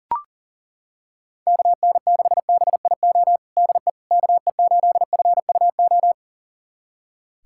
Morse code, heard as a keyed tone, about 40 wpm, 700 Hz. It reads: KN6BIO DE KE8FUO
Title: USA Callsign DE Callsign - 6 Characters - 40wpm